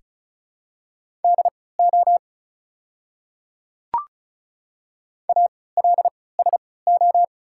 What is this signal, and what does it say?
Morse code, heard as a keyed tone, about 36 wpm, spaced as for 20 wpm, 700 Hz. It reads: DO ALSO
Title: 3x Character Spacing - Top 200 Words - 35wpm